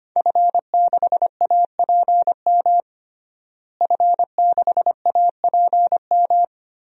Morse code, heard as a keyed tone, about 25 wpm, 700 Hz. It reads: F6APM F6APM
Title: DXCC-100 Call Signs - 5 Characters - Repeated 3x - 25wpm